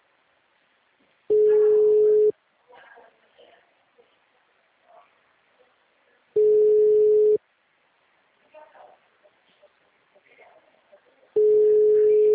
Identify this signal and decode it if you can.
Ringback tone (Europe, 425 Hz), with speech.